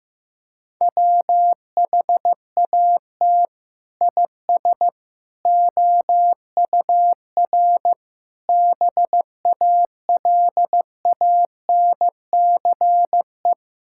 Morse code, heard as a keyed tone, about 15 wpm, 700 Hz. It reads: WHAT IS OUR BALANCE
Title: Sentences from Top 1000 Words - Review Part 6 - 15wpm